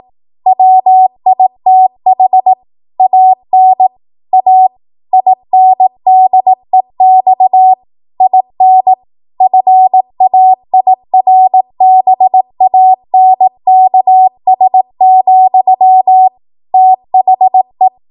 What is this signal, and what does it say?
Morse code, heard as a keyed tone, about 18 wpm, 750 Hz. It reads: WITH AN A INDEX IN FAIRBANKS, THE
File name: Archive / 18 WPM / 18WPM